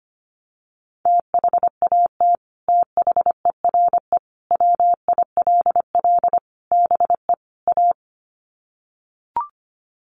Morse code, heard as a keyed tone, about 25 wpm, 700 Hz. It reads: THAT THERE WILL BE A
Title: Sets of 5 Words Encore2 25wpm